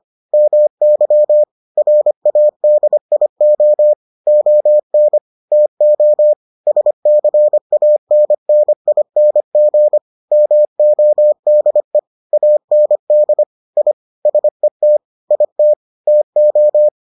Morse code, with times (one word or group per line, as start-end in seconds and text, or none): 0.3-1.4 MY
1.8-3.9 RADIO
4.3-5.2 ON
5.5-6.3 TO
6.7-10.0 SCANNING
10.3-12.0 MODE
12.3-13.4 AND
13.8-13.9 I
14.3-15.0 SET
15.3-15.7 IT
16.1-16.9 TO